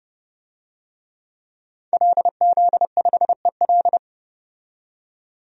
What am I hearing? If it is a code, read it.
Morse code, heard as a keyed tone, about 30 wpm, 700 Hz. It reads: LZ5EL
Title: DXCC-100 Call Signs - 5 Characters - 30wpm